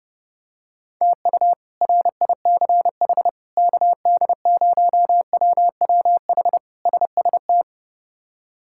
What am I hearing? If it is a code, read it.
Morse code, heard as a keyed tone, about 30 wpm, 700 Hz. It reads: TU RICH KD0WWH SST
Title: K1USN SST QSO Practice - 30wpm